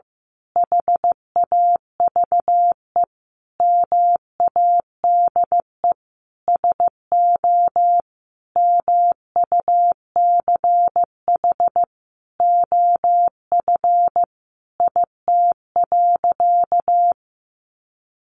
Morse code, heard as a keyed tone, about 15 wpm, 700 Hz. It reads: HAVE MADE SO MUCH OF IT.
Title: Sentences from Top 1400 Words 15wpm